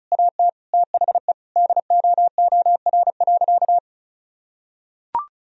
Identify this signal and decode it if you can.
Morse code, heard as a keyed tone, about 36 wpm, 700 Hz. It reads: AT THE DOOR.
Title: Sentences from Top 1000 Words - Review Part 2 - 35wpm